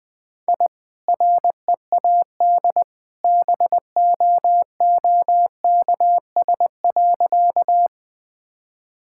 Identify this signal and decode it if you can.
Morse code, heard as a keyed tone, about 21 wpm, 700 Hz. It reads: I READ BOOKS.
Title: Easy Sentences with Hints - 20wpm